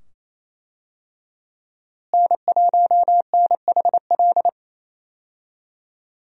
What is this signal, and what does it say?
Morse code, heard as a keyed tone, about 28 wpm, 700 Hz. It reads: N1NHL
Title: USA Call Signs - 5 Characters - Repeated 2x - 28wpm